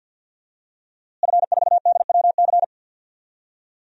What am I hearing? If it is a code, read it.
Morse code, heard as a keyed tone, about 52 wpm, 700 Hz. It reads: F4DWC
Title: DXCC-100 Call Signs - 5 Characters - Repeated 2x - 50wpm